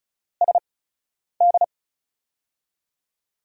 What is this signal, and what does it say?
Morse code, heard as a keyed tone, about 36 wpm, spaced as for 10 wpm, 700 Hz. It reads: SD
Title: USA Call Signs - 8x Character Spacing - 35wpm